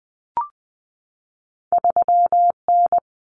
Morse code, heard as a keyed tone, about 20 wpm, 700 Hz. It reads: 3N
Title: Common Cut Numbers - 2-Digit 20wpm